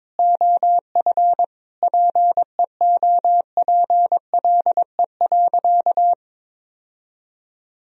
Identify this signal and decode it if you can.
Morse code, heard as a keyed tone, about 22 wpm, 700 Hz. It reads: OF PEOPLE.
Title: Sentences from Top 1000 Words - Review Part 2 - 22wpm